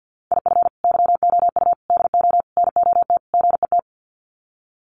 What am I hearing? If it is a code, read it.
Morse code, heard as a keyed tone, about 50 wpm, 700 Hz. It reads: IF YOU DO NOT GET
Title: Sets of 5 Words Encore2 50wpm